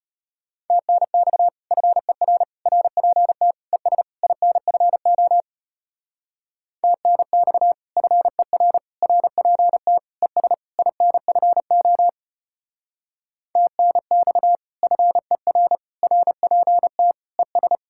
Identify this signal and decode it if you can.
Morse code, heard as a keyed tone, about 32 wpm, 700 Hz. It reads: TNX FER RPT ES INFO TNX FER RPT ES INFO TNX FER RPT ES